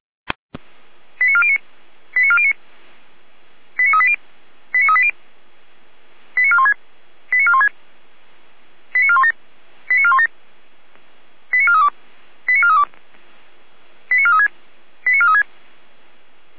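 Five-tone selective call (ZVEI), heard as quick runs of five tones, tones of about 70 ms. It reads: 89409 89409 89380 89380 89416 89416 89417 89417 89432 89432 89436 89436
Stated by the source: ZVEI five-tone call received by SDR, recorded by Feuerwehr Orlamünde.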